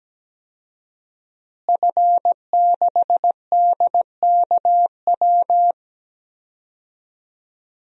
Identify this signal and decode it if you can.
Morse code, heard as a keyed tone, about 17 wpm, 700 Hz. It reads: F6DKW
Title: DXCC-100 Call Signs - 5 Characters - Repeated 3x - 17wpm